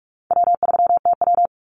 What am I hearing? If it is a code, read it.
Morse code, heard as a keyed tone, about 44 wpm, 700 Hz. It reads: W3TW
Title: Call Signs - 45wpm